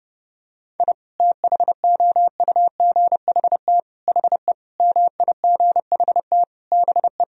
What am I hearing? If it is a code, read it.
Morse code, heard as a keyed tone, about 32 wpm, 700 Hz. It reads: I THOUGHT HE MIGHT BE
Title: Sentences from Top 1000 Words - Review Part 4 - 30wpm